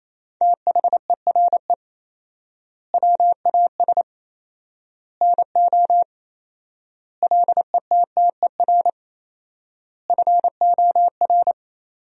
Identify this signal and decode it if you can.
Morse code, heard as a keyed tone, about 27 wpm, 700 Hz. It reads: THERE WAS NO LETTER FOR
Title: 4x Word-Spacing - Sentences from Top 300 Words - 28wpm